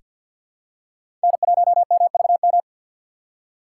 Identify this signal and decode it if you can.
Morse code, heard as a keyed tone, about 47 wpm, 700 Hz. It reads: N1MVM
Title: USA Call Signs - 6x Character Spacing - 50wpm